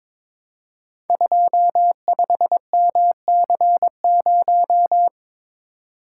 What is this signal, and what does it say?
Morse code, heard as a keyed tone, about 21 wpm, 700 Hz. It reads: 25MC0
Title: Code Groups - 5 Letter:Numbers - 22wpm